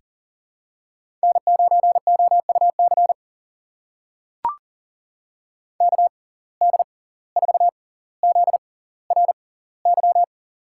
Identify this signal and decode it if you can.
Morse code, heard as a keyed tone, about 40 wpm, spaced as for 19 wpm, 700 Hz. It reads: N9OUC KD4ZRY
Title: USA Call Signs - 6x Character Spacing - 40wpm